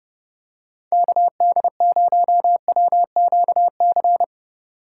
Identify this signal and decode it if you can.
Morse code, heard as a keyed tone, about 28 wpm, 700 Hz. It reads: KD0WQC